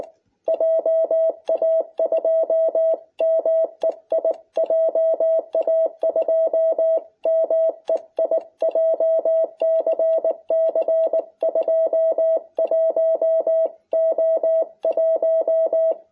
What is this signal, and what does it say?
Morse code, heard as a keyed tone, about 19 wpm, 660 Hz. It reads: JA2MEI JA2MEI JCC 21O1